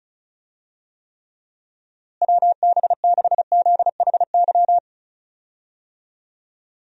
Morse code, heard as a keyed tone, about 34 wpm, 700 Hz. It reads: WB6ZHY
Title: USA Call Signs - 6 Characters - Repeated 3x - 35wpm